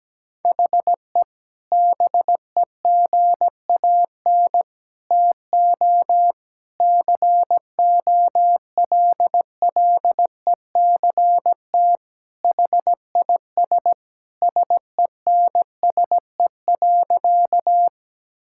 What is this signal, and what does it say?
Morse code, heard as a keyed tone, about 17 wpm, 700 Hz. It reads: HE BEGAN TO COLLECT HIS SENSE.
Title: Sentences from Top 1600 Words 17wpm